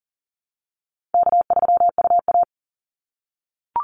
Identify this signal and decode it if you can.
Morse code, heard as a keyed tone, about 40 wpm, 700 Hz. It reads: K3UA
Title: Call Signs - 40wpm